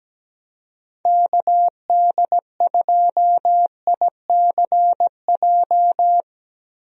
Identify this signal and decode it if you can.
Morse code, heard as a keyed tone, about 17 wpm, 700 Hz. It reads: KD2ICJ
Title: USA Call Signs - 6 Characters - Repeated 3x - 17wpm